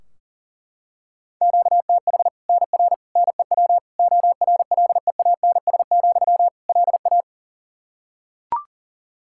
Morse code, heard as a keyed tone, about 40 wpm, 700 Hz. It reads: QTH NR NEW ORLEANS, LA
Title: Practice QSO - Exchange 1 - Beginning - 40wpm